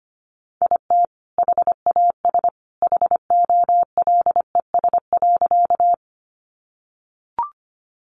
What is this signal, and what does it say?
Morse code, heard as a keyed tone, about 25 wpm, 700 Hz. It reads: IT HAS HOLES.
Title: Additional Easy Phrases 25wpm